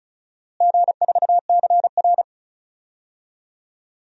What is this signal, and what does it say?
Morse code, heard as a keyed tone, about 36 wpm, 700 Hz. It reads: G4CR
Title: DXCC-100 Call Signs - 4 Characters - Repeated 3x - 35wpm